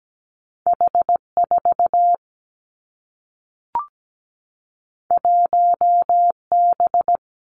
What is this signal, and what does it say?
Morse code, heard as a keyed tone, about 17 wpm, 700 Hz. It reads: H4 1B
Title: Two Challenging Letter-Number Combinations 17wpm